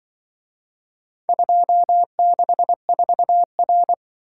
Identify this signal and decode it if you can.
Morse code, heard as a keyed tone, about 23 wpm, 700 Hz. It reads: 264R
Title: Speed-Racing - Four Letter-Number Combinations - 20wpm